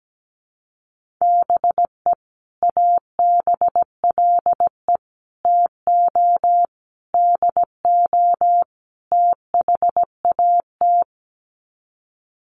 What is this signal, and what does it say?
Morse code, heard as a keyed tone, about 17 wpm, 700 Hz. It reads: BE ABLE TO DO THAT
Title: Sets of 5 Words Encore2 17wpm